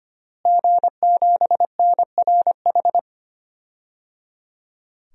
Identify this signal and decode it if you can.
Morse code, heard as a keyed tone, about 25 wpm, 700 Hz. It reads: G7NRH
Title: DXCC-100 Call Signs - 5 Characters - Repeated 3x - 25wpm